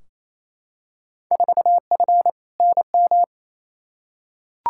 Morse code, heard as a keyed tone, about 28 wpm, 700 Hz. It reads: 4F NM
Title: Field Day Exchanges Encore - 28wpm